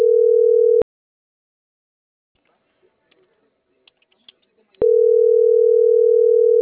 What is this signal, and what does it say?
Ringback tone (North America), 440 + 480 Hz.